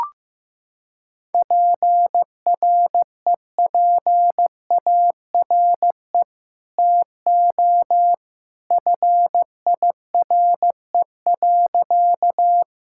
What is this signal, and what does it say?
Morse code, heard as a keyed tone, about 15 wpm, 700 Hz. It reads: PREPARE TO FIRE.
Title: Sentences from Top 1000 Words - Review Part 6 - 15wpm